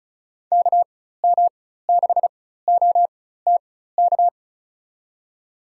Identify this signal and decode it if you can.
Morse code, heard as a keyed tone, about 34 wpm, spaced as for 16 wpm, 700 Hz. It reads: KM6OTK
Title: USA Call Signs - 6 Characters - 4x Character Spacing - 35wpm